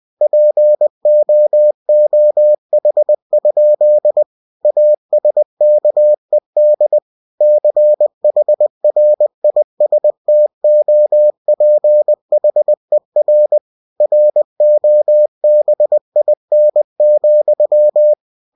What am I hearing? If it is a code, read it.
Morse code, heard as a keyed tone, about 20 wpm, 600 Hz. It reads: POOH? ASKED CHRISTOPHER ROBIN,
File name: cw-books / winnie-the-pooh / winnie-20wpm-page-0098